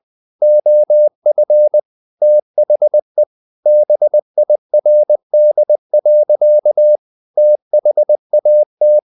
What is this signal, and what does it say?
Morse code, heard as a keyed tone, about 20 wpm, 600 Hz. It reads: OF THE BIRD. THAT
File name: the old man and the sea / the old man and the sea part 1 20wpm